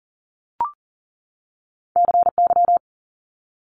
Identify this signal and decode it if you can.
Morse code, heard as a keyed tone, about 41 wpm, 700 Hz. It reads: CY
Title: Multi-letter Phonemes 40wpm